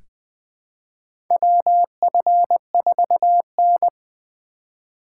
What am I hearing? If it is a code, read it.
Morse code, heard as a keyed tone, about 20 wpm, 700 Hz. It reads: WF4N
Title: USA Call Signs - 4 Characters - 2x Character Spacing - 20wpm